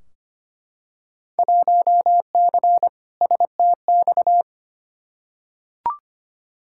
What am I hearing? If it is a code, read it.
Morse code, heard as a keyed tone, about 25 wpm, 700 Hz. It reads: 1C STX
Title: Field Day Exchanges Encore - 25wpm